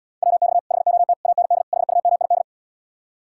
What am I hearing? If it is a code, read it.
Morse code, heard as a keyed tone, about 77 wpm, 700 Hz. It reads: WX HOT AND HUMID